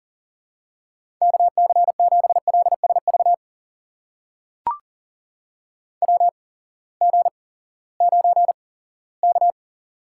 Morse code, heard as a keyed tone, about 42 wpm, spaced as for 20 wpm, 700 Hz. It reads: KC7LSV WG9K
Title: USA Call Signs - 8x Character Spacing - 40wpm